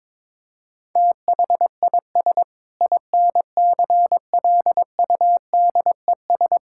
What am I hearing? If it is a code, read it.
Morse code, heard as a keyed tone, about 22 wpm, 700 Hz. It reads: THIS INCLUDES
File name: Sentences with 7-Word Phrases - 22wpm